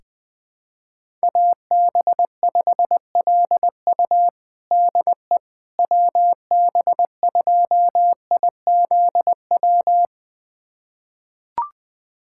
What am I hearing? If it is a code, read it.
Morse code, heard as a keyed tone, about 21 wpm, 700 Hz. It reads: AB5LU DE WB2IZW
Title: USA Callsign DE Callsign 20wpm